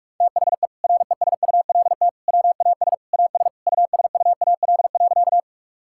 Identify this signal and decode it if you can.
Morse code, heard as a keyed tone, about 46 wpm, 700 Hz. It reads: THE RESULT WAS AS USUAL.
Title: Sentences from Top 1000 Words - Review Part 5 - 45wpm